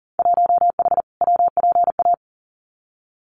Morse code, heard as a keyed tone, about 40 wpm, 700 Hz. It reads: 1H WPA
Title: Winter Field Day Exchanges 40wpm